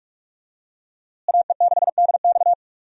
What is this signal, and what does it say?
Morse code, heard as a keyed tone, about 46 wpm, 700 Hz. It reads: AE6DX
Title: USA Call Signs - 5 Characters - Repeated 3x - 45wpm